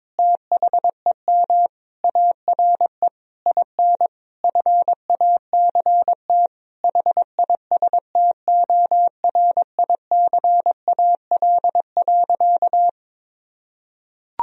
Morse code, heard as a keyed tone, about 22 wpm, 700 Hz. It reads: THEM ARE IN FACT HISTORICAL.
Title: Sentences from Top 1000 Words - Review Part 4 - 22wpm